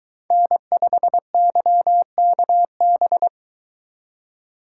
Morse code, heard as a keyed tone, about 22 wpm, 700 Hz. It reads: N5YKB